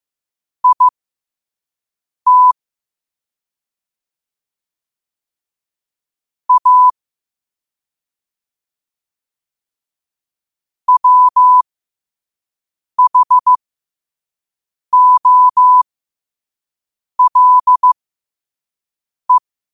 Morse code, heard as a keyed tone, about 15 wpm, spaced as for 5 wpm, 1000 Hz. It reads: IT A WHOLE
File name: qotdfiles / 5wpm